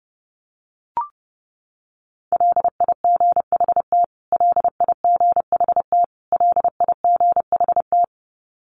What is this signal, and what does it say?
Morse code, heard as a keyed tone, about 30 wpm, 700 Hz. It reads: LIGHT LIGHT LIGHT
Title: Top 200 Words Repeated 30wpm